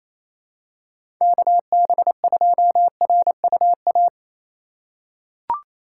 Morse code, heard as a keyed tone, about 28 wpm, 700 Hz. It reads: KB2RUA
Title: USA Call Signs - 2x Character Spacing - 28wpm